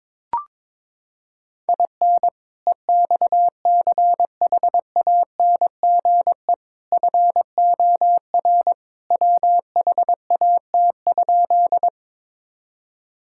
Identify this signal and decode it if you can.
Morse code, heard as a keyed tone, about 21 wpm, 700 Hz. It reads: IN EXCHANGE FOR WHAT?
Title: Sentences from Top 1000 Words - Review Part 6 - 22wpm